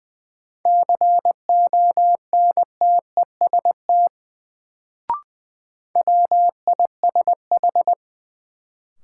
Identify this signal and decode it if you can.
Morse code, heard as a keyed tone, about 20 wpm, 700 Hz. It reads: CONTEST WISH
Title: Sentences from Top 1600 Words - New Word Drill - 20wpm